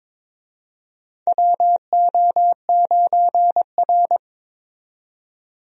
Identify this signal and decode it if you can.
Morse code, heard as a keyed tone, about 22 wpm, 700 Hz. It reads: WO9R
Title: USA Call Signs - 4 Characters - 22wpm